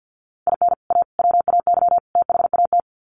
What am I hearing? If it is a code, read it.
Morse code, heard as a keyed tone, about 50 wpm, 700 Hz. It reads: IN A WAY THAT
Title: Sets of 4 Words 50wpm